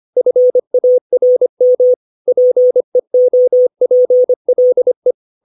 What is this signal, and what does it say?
Morse code, heard as a keyed tone, about 25 wpm, 500 Hz. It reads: FARM PEOPLE
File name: 632 words / 632 Words 25WPM